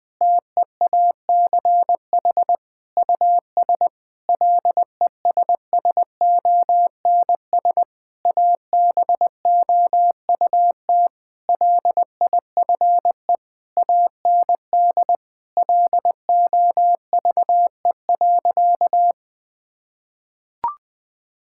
Morse code, heard as a keyed tone, about 20 wpm, 700 Hz. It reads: TEACH US LESSONS ABOUT LIFE AND LOVE.
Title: Difficult Sentences with Hints - 20wpm